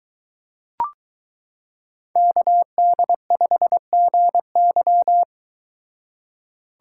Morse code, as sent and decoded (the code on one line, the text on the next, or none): -.- -.. ..... --. -.--
KD5GY